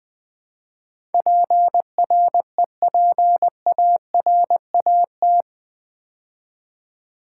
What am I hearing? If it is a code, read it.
Morse code, heard as a keyed tone, about 20 wpm, 700 Hz. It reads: PREPARAT
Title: Build A Word - 4 Letters At A Time - 11-Letter Words - 20wpm